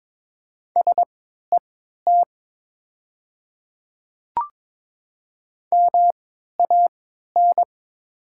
Morse code, heard as a keyed tone, about 22 wpm, spaced as for 13 wpm, 700 Hz. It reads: SET MAN
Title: 3x Character Spacing - Top 200 Words - 22wpm